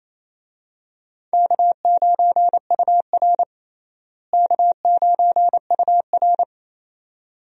Morse code, heard as a keyed tone, about 28 wpm, 700 Hz. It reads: K9UR K9UR
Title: USA Call Signs - 4 Characters - Repeated 2x - 28wpm